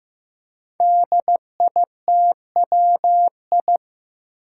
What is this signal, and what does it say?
Morse code, heard as a keyed tone, about 15 wpm, 700 Hz. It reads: DITWI
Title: Five Challenging Letter-Number Combinations 15wpm